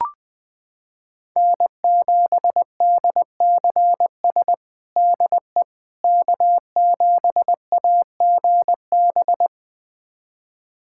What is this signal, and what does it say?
Morse code, heard as a keyed tone, about 21 wpm, 700 Hz. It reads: N7DCS DE K7AGB
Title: USA Callsign DE Callsign 20wpm